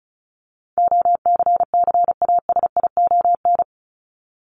Morse code, heard as a keyed tone, about 34 wpm, 700 Hz. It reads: OCCASION
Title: Top 1501-1750 Words 35wpm